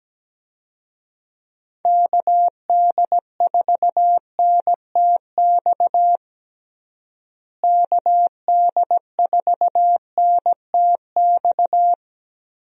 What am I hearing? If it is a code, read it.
Morse code, heard as a keyed tone, about 17 wpm, 700 Hz. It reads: KD4NTX KD4NTX
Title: USA Call Signs - 6 Characters - Repeated 2x - 17wpm